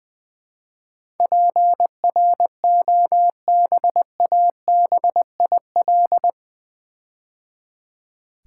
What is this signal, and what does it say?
Morse code, heard as a keyed tone, about 20 wpm, 700 Hz. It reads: PROBABIL
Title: Build A Word - 4 Letters At A Time - 11-Letter Words - 20wpm